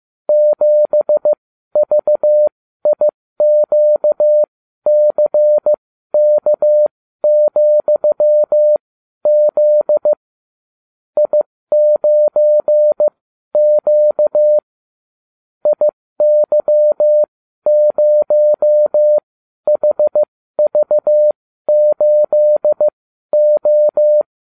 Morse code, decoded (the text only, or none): 7VIQCK,Z I9Q IY0HV8O